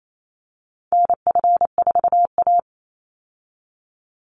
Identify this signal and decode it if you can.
Morse code, heard as a keyed tone, about 28 wpm, 700 Hz. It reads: NF4A
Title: Call Signs Repeated 3x - 28wpm